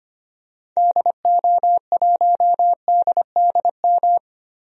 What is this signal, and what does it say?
Morse code, heard as a keyed tone, about 25 wpm, 700 Hz. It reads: DO1DDM
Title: DXCC-100 Call Signs - 6 Characters - 25wpm